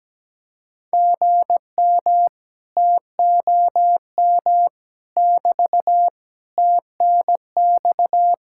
Morse code, heard as a keyed tone, about 17 wpm, 700 Hz. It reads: GM TOM <BT> TNX
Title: Speed-Racing Common QSO Phrases - 17wpm